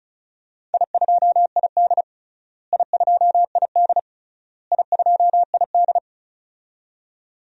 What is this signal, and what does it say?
Morse code, heard as a keyed tone, about 34 wpm, 700 Hz. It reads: I2ID I2ID I2ID
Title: DXCC-100 Call Signs - 4 Characters - Repeated 3x - 35wpm